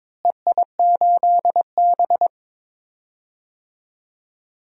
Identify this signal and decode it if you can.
Morse code, heard as a keyed tone, about 22 wpm, 700 Hz. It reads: EI8B
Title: DXCC-100 Call Signs - 4 Characters - Repeated 3x - 22wpm